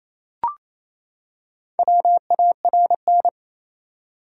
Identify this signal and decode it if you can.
Morse code, heard as a keyed tone, about 28 wpm, 700 Hz. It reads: WARN
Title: Four Letter Words - 28wpm